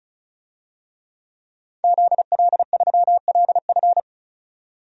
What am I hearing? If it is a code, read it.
Morse code, heard as a keyed tone, about 36 wpm, 700 Hz. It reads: ZL3LF
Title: DXCC-100 Call Signs - 5 Characters - 35wpm